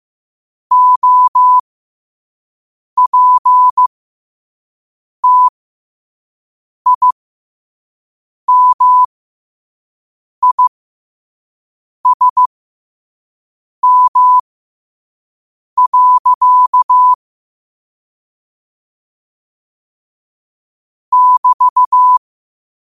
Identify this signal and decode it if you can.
Morse code, heard as a keyed tone, about 15 wpm, spaced as for 5 wpm, 1000 Hz. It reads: OPTIMISM. <BT>
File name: qotdfiles / 5wpm